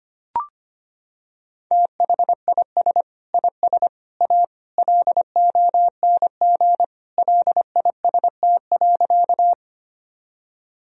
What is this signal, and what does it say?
Morse code, heard as a keyed tone, about 25 wpm, 700 Hz. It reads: THIS IS A LONG LIST.
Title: Sentences from Top 1000 Words - Review Part 2 - 25wpm